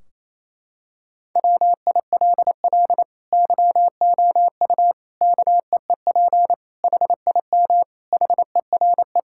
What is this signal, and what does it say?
Morse code, heard as a keyed tone, about 28 wpm, 700 Hz. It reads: WILL YOU KEEP HIM HERE
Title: Sentences from Top 1600 Words 28wpm